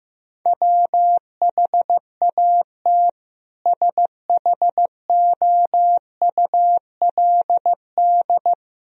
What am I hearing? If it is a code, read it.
Morse code, heard as a keyed tone, about 15 wpm, 700 Hz. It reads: WHAT SHOULD
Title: Sentences from Top 1000 Words - Review Part 4 - 15wpm